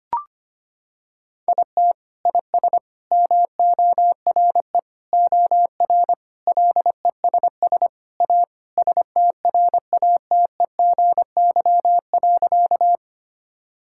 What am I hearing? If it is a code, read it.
Morse code, heard as a keyed tone, about 25 wpm, 700 Hz. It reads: IT IS MORE OR LESS A STRATEGY.
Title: Sentences from Top 1000 Words - Review Part 5 - 25wpm